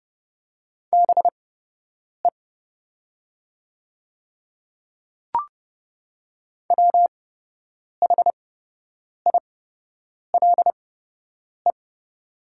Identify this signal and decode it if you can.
Morse code, heard as a keyed tone, about 30 wpm, spaced as for 8 wpm, 700 Hz. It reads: BE WHILE
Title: 8x Character Spacing - Top 300 Words - 30wpm